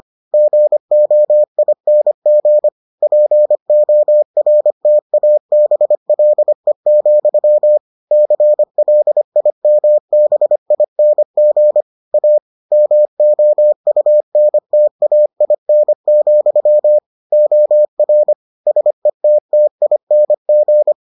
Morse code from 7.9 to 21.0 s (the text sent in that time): CLIMBING A MOUNTAIN, OR SETTING